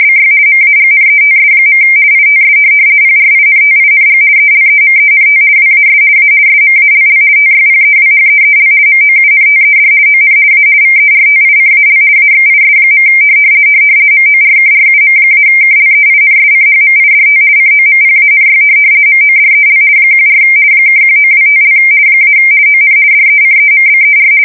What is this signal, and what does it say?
RTTY, about 50 baud. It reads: BY MAGNETS. ITS PRINTING ELEMENT WAS A SMALL TYPEWHEEL MOUNTED ON A VERTICAL AXIS. MORKRUM-KLEINSCHMIDT COMPANY MODEL 14 (1925) - ON DECEMBER
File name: fsk170-50B / teletype-corp